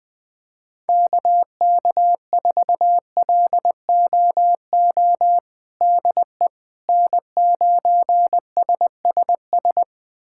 Morse code, heard as a keyed tone, about 20 wpm, 700 Hz. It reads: KK4LOO DE N9SSS
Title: USA Callsign DE Callsign 20wpm